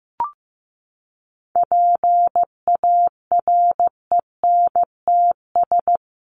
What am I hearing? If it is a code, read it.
Morse code, heard as a keyed tone, about 15 wpm, 700 Hz. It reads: PARENTS
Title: ES:S Suffix - Top 100 Words with Base Words 15wpm